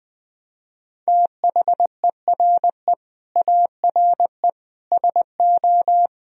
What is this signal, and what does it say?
Morse code, heard as a keyed tone, about 21 wpm, 700 Hz. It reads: THERE ARE SO
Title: Speed-Racing - Sets of 4 Words 20wpm